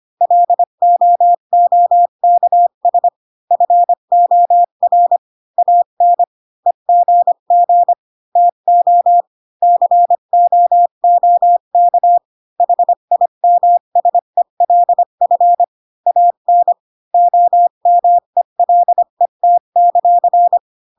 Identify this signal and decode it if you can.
Morse code, heard as a keyed tone, about 25 wpm, 700 Hz. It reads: LOOKS FOR AN EGG TO COOK HIMSELF AN OMELET;
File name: Pinocchio / Pinocchio ch5 20 25 ik1hge